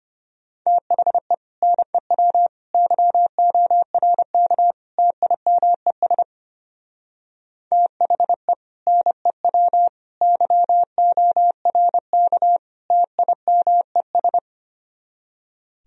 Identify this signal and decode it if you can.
Morse code, heard as a keyed tone, about 32 wpm, 700 Hz. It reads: THE NEW YORK TIMES THE NEW YORK TIMES